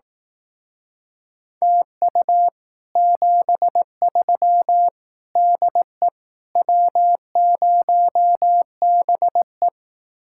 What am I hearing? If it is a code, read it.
Morse code, heard as a keyed tone, about 18 wpm, 700 Hz. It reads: TU 73 DE W0BE
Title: Speed-Racing Common QSO Phrases Encore - 15wpm